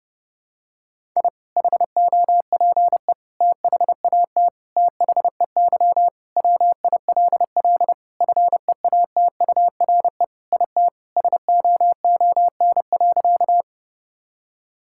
Morse code, heard as a keyed tone, about 32 wpm, 700 Hz. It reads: I HOPE THAT THEY WILL FEATURE IT SOON.
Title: Sentences from Top 1000 Words - Review Part 5 - 30wpm